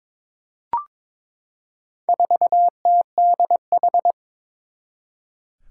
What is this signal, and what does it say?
Morse code, heard as a keyed tone, about 22 wpm, 700 Hz. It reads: 4TDH
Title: Four Challenging Letter-Number Combinations 22wpm